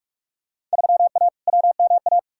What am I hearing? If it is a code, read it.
Morse code, heard as a keyed tone, about 45 wpm, 700 Hz. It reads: 3A WMA
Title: Field Day Exchanges Encore - 45wpm